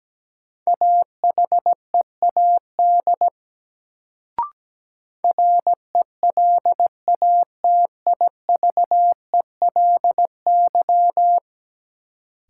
Morse code, heard as a keyed tone, about 17 wpm, 700 Hz. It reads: AHEAD RELATIVELY
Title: Sentences from Top 1100 Words - New Word Drill Alt - 17wpm